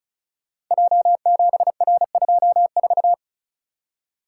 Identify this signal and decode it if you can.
Morse code, heard as a keyed tone, about 36 wpm, 700 Hz. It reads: J7R24
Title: Code Groups - 5 Letter:Numbers - 35wpm